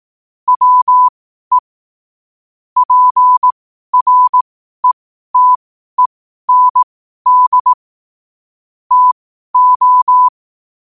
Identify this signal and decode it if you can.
Morse code, heard as a keyed tone, about 18 wpm, spaced as for 12 wpm, 1000 Hz. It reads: WE PRETEND TO
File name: qotdfiles / 12wpm